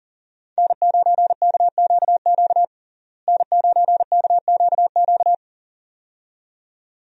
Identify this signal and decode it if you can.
Morse code, heard as a keyed tone, about 40 wpm, 700 Hz. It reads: N9KQQ N9KQQ
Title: USA Call Signs - 5 Characters - Repeated 2x - 40wpm